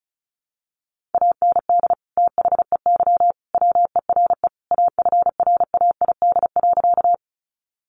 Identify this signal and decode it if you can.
Morse code, heard as a keyed tone, about 34 wpm, 700 Hz. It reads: AND THEY WERE AFRAID.
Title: Sentences from Top 1500 Words 35wpm